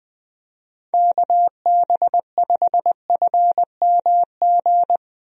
Morse code, heard as a keyed tone, about 21 wpm, 700 Hz. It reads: KB5FMG
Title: USA Call Signs - 6 Characters - 20wpm